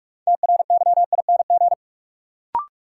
Morse code, heard as a keyed tone, about 46 wpm, 700 Hz. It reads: TRYING
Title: Sets of 2 Words - New Word Drill - 45wpm